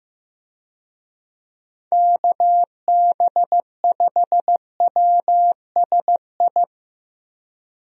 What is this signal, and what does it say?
Morse code, heard as a keyed tone, about 15 wpm, 700 Hz. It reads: KB5WSI
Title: USA Call Signs - 6 Characters - Repeated 3x - 15wpm